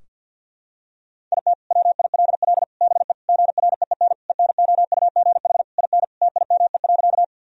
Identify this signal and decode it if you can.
Morse code, heard as a keyed tone, about 50 wpm, 700 Hz. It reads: IT WILL BE GREEN ENOUGH IN TIME.
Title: Sentences from Top 1000 Words - Review Part 2 - 50wpm